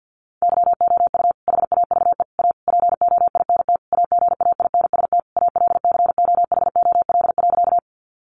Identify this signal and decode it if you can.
Morse code, heard as a keyed tone, about 50 wpm, 700 Hz. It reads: YOU HAVE A POINT AGAINST ALCOHOL.
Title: Sentences from Top 1400 Words 50wpm